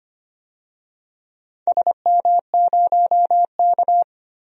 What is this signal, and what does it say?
Morse code, heard as a keyed tone, about 25 wpm, 700 Hz. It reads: SM0K
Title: DXCC-100 Call Signs - 4 Characters - 25wpm